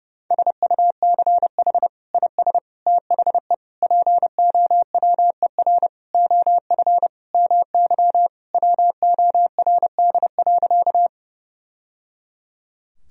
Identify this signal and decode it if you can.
Morse code, heard as a keyed tone, about 30 wpm, 700 Hz. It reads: SUCH IS THE POWER OF MY WORD.
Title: Sentences from Top 1000 Words - Review Part 2 - 30wpm